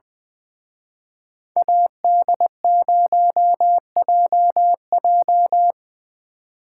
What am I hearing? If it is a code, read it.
Morse code, heard as a keyed tone, about 24 wpm, 700 Hz. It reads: AD0JJ
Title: Speed Racing - USA Call Signs 20wpm